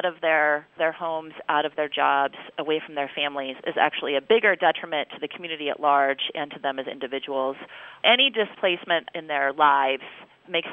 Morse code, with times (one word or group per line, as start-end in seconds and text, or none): none